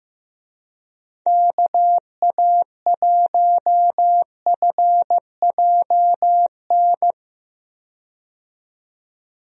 Morse code, heard as a keyed tone, about 15 wpm, 700 Hz. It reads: KA1FJN
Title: USA Call Signs - 6 Characters - 15wpm